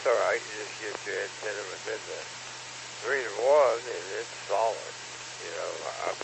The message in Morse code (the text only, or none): none